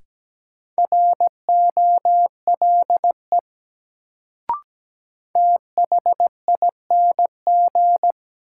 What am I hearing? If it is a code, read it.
Morse code, heard as a keyed tone, about 17 wpm, 700 Hz. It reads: ROLE THING
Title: Sets of 3 Words - New Word Drill - 17wpm